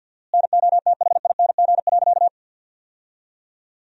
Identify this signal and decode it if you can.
Morse code, heard as a keyed tone, about 48 wpm, 700 Hz. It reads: NOTHING.
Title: 4x Word-Spacing - Sentences from Top 500 Words - 50wpm